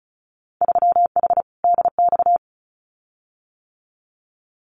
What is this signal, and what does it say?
Morse code, heard as a keyed tone, about 36 wpm, 700 Hz. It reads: 3H DX
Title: Winter Field Day Exchanges Encore 35wpm